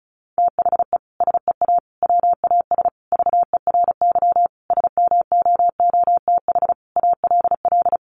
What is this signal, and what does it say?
Morse code, heard as a keyed tone, about 34 wpm, 700 Hz. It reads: THE SEA WAS VERY SMOOTH ALL
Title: Sentences from Top 1400 Words 35wpm